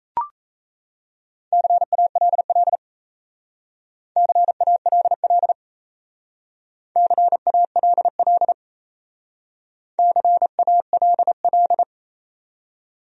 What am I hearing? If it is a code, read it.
Morse code, heard as a keyed tone, about 36 wpm, 700 Hz. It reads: CALL CALL CALL CALL